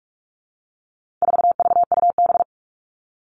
Morse code, heard as a keyed tone, about 44 wpm, 700 Hz. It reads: 4VUB
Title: All Cut Numbers - 4-Digit 45wpm